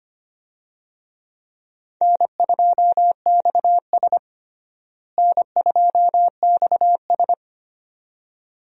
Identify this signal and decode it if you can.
Morse code, heard as a keyed tone, about 25 wpm, 700 Hz. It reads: N2XS N2XS
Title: USA Call Signs - 4 Characters - Repeated 2x - 25wpm